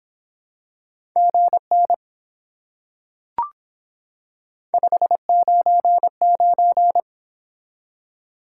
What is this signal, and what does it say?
Morse code, heard as a keyed tone, about 25 wpm, 700 Hz. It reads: GN 599